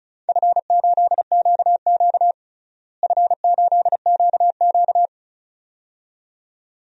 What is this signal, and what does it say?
Morse code, heard as a keyed tone, about 34 wpm, 700 Hz. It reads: F8QQ F8QQ
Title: DXCC-100 Call Signs - 4 Characters - Repeated 2x - 35wpm